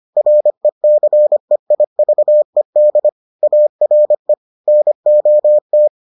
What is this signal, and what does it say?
Morse code, heard as a keyed tone, about 25 wpm, 600 Hz. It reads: RECEIVED ARE NOT